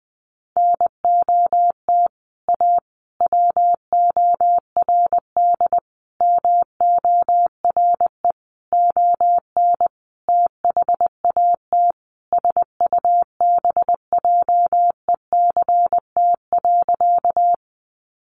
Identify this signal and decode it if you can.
Morse code, heard as a keyed tone, about 20 wpm, 700 Hz. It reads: NOT A WORD MORE ON THAT SUBJECT.
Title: Sentences from Top 800 Words 20wpm